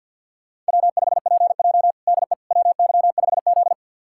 Morse code, heard as a keyed tone, about 49 wpm, 700 Hz. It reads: W5PJ DE WY5Z
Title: USA Callsign DE Callsign - 4 Characters - 50wpm